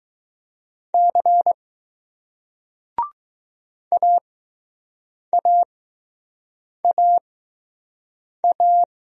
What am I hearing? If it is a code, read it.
Morse code, heard as a keyed tone, about 20 wpm, spaced as for 6 wpm, 700 Hz. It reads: C AAAA